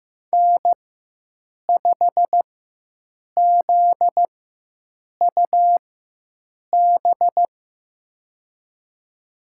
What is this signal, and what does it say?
Morse code, heard as a keyed tone, about 15 wpm, spaced as for 7 wpm, 700 Hz. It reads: N5ZUB
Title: USA Call Signs - 4x Character Spacing - 15wpm